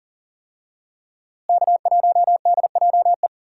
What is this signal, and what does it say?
Morse code, heard as a keyed tone, about 40 wpm, 700 Hz. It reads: K1DJE
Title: USA Call Signs - 5 Characters - Repeated 3x - 40wpm